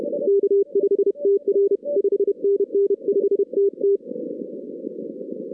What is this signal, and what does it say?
Morse code, heard as a keyed tone, about 31 wpm, 400 Hz. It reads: K5TR 5NN 5TT
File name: 2016 arrl dx cw / 2016-ARRL-DX-CW-K5TR- - 0-41